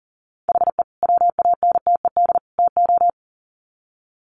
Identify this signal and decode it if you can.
Morse code, heard as a keyed tone, about 40 wpm, 700 Hz. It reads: HE WANTED TO
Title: Sets of 3 Words 40wpm